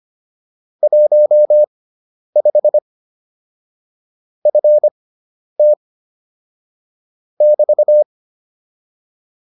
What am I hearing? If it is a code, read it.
Morse code, heard as a keyed tone, about 25 wpm, spaced as for 10 wpm, 600 Hz. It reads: 15 FT <BT>
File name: QSO302 10